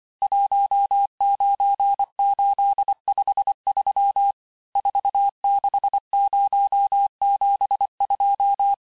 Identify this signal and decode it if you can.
Morse code, heard as a keyed tone, about 25 wpm, 800 Hz. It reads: 19853 46072
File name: texte-chiffre-vitesse-23mpm